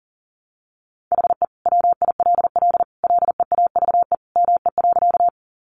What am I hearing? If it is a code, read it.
Morse code, heard as a keyed tone, about 40 wpm, 700 Hz. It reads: HE WILL LEAVE ME.
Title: Sentences from Top 400 Words 40wpm